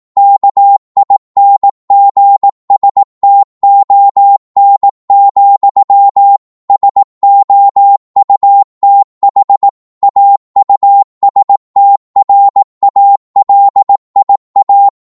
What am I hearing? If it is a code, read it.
Morse code, heard as a keyed tone, about 18 wpm, 800 Hz. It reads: KINGSTON, SOUTH AUSTRALIA
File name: Morse / qso-18wpm